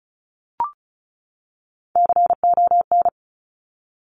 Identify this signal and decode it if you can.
Morse code, heard as a keyed tone, about 36 wpm, 690 Hz. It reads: CON
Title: Top Three Letter Combinations in Words 35wpm